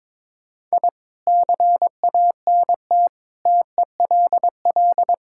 Morse code, heard as a keyed tone, about 22 wpm, 700 Hz. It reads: I CANT TELL
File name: Sentences with 7-Word Phrases - 22wpm